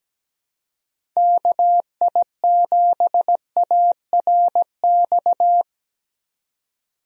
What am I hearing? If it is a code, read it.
Morse code, heard as a keyed tone, about 17 wpm, 700 Hz. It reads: KI7ARX
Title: USA Call Signs - 6 Characters - 17wpm